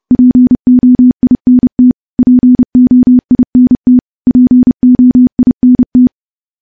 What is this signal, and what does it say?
Morse code, heard as a keyed tone, about 31 wpm, 250 Hz. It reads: POINT POINT POINT
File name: morse / morse-training-30WPM